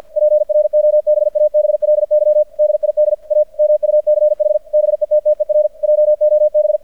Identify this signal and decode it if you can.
Morse code, heard as a keyed tone, about 50 wpm, 600 Hz. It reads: PROBABLY DID A MUCH BETTER JOB
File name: archived / CRSnet